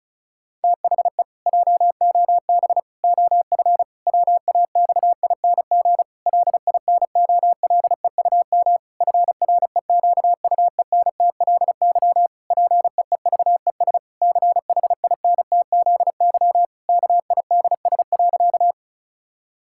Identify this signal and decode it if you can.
Morse code, heard as a keyed tone, about 36 wpm, 700 Hz. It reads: THE JOB OF WAXING LINOLEUM FREQUENTLY PEEVES CHINTZY KIDS.
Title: Pangram Challenge v2 - 35wpm